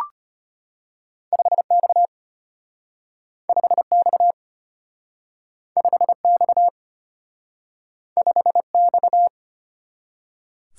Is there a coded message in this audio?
Yes: Morse code from 1.3 to 9.3 s, heard as a keyed tone, about 33 wpm, 700 Hz. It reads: 5X 5X 5X 5X